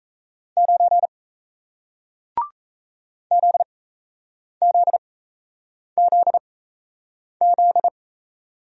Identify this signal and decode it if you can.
Morse code, heard as a keyed tone, about 38 wpm, spaced as for 8 wpm, 700 Hz. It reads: 9 ZZZZ